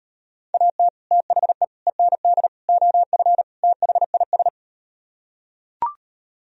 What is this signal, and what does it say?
Morse code, heard as a keyed tone, about 38 wpm, 700 Hz. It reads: AT THE END OF THIS